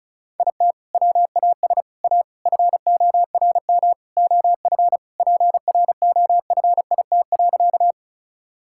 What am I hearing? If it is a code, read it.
Morse code, heard as a keyed tone, about 36 wpm, 700 Hz. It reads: IT WAS A FORM OF PROFIT.
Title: Sentences from Top 1000 Words - Review Part 5 - 35wpm